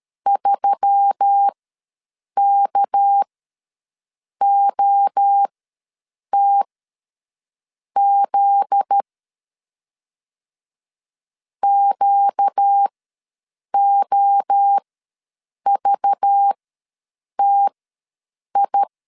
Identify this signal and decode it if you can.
Morse code, heard as a keyed tone, about 13 wpm, spaced as for 7 wpm, 800 Hz. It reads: 3KOTZ QOVTI
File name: LEZIONI / 7 GF9Z / 7 GF9Z ws 12wpm 3321